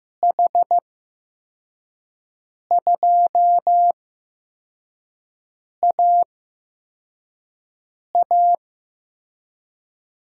Morse code, heard as a keyed tone, about 15 wpm, spaced as for 4 wpm, 700 Hz. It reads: H2AA
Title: USA Call Signs - 5 Characters - 8x Character Spacing - 15wpm